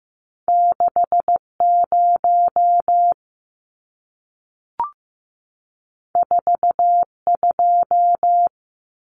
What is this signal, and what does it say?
Morse code, heard as a keyed tone, about 15 wpm, 700 Hz. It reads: 60 42
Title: Numbers - 2-Digit 15wpm